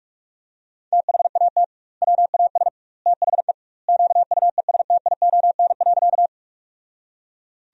Morse code, heard as a keyed tone, about 41 wpm, 700 Hz. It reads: THAT WAS THE QUESTION.
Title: 2x Word-Spacing - Sentences from Top 400 Words - 45wpm